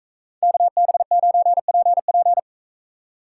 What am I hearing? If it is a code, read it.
Morse code, heard as a keyed tone, about 40 wpm, 700 Hz. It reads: KB9PP